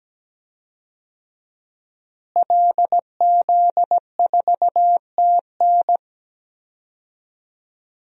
Morse code, heard as a keyed tone, about 17 wpm, 700 Hz. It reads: LZ4TN
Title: DXCC-100 Call Signs - 5 Characters - Repeated 3x - 17wpm